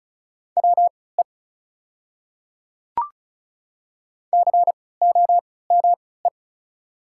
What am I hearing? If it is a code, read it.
Morse code, heard as a keyed tone, about 36 wpm, spaced as for 20 wpm, 700 Hz. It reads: WE COME
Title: 3x Character Spacing - Top 100 Words - 35wpm